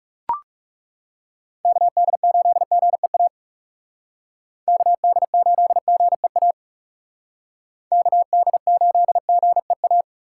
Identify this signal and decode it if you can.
Morse code, heard as a keyed tone, about 34 wpm, 700 Hz. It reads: KD8GEA KD8GEA KD8GEA
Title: Speed Racing - USA 2x3 Call Signs 30wpm